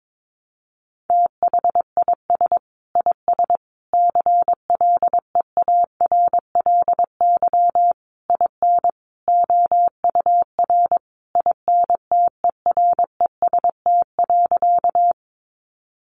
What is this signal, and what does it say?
Morse code, heard as a keyed tone, about 22 wpm, 700 Hz. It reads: THIS IS CLEARLY IN OUR INTEREST.
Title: Sentences from Top 1300 Words 22wpm